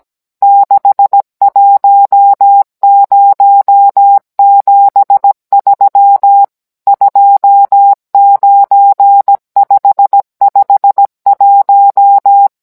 Morse code, heard as a keyed tone, about 17 wpm, 800 Hz. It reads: 61073 29551